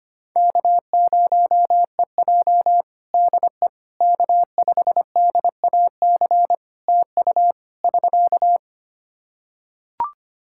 Morse code, heard as a keyed tone, about 25 wpm, 700 Hz. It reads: K0EJ DE K5DAC TU <SK>
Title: Practice QSO - Exchange 4 - Ending - 25wpm